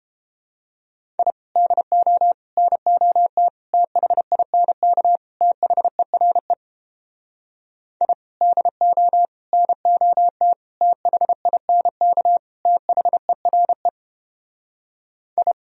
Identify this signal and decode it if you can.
Morse code, heard as a keyed tone, about 31 wpm, 700 Hz. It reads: I DO NOT THINK THERE I DO NOT THINK THERE I